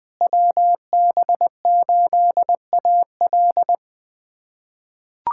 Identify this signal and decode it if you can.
Morse code, heard as a keyed tone, about 20 wpm, 700 Hz. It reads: WB8AL
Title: USA Callsign DE Callsign 20wpm